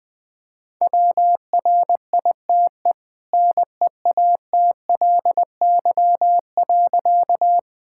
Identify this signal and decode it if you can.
Morse code, heard as a keyed tone, about 20 wpm, 700 Hz. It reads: WRITE NEATLY.
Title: Easy Sentences with Hints - 20wpm